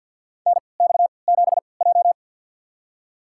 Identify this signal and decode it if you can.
Morse code, heard as a keyed tone, about 47 wpm, spaced as for 28 wpm, 700 Hz. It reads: NX7J
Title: USA Call Signs - 4 Characters - 3x Character Spacing - 50wpm